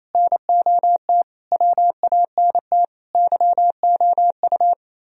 Morse code, heard as a keyed tone, about 34 wpm, 700 Hz. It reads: NOT WANT YOU
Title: Speed-Racing - Sets of 5 Words 28wpm